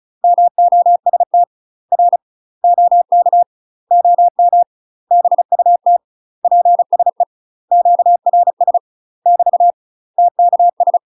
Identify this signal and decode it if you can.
Morse code, heard as a keyed tone, about 34 wpm, 700 Hz. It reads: MOST R OK OM BUT PSE QRS <BT> TKS
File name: UK MASTER QSOs TRAINING / 35WPM / UK MASTER QSOs 0024